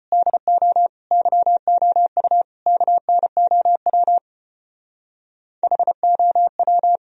Morse code, heard as a keyed tone, about 32 wpm, 690 Hz. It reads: DO YOU KNOW HOW